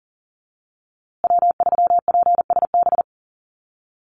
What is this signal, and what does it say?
Morse code, heard as a keyed tone, about 40 wpm, 700 Hz. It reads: W3PSB
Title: USA Call Signs Repeated 2x 40wpm